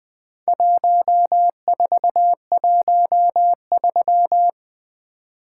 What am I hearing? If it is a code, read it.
Morse code, heard as a keyed tone, about 22 wpm, 700 Hz. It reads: 1413
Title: Speed-Racing - Four Letter-Number Combinations - 20wpm